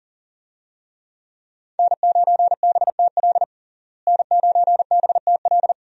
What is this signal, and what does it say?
Morse code, heard as a keyed tone, about 40 wpm, 700 Hz. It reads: N9BTL N9BTL
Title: USA Call Signs - 5 Characters - Repeated 3x - 40wpm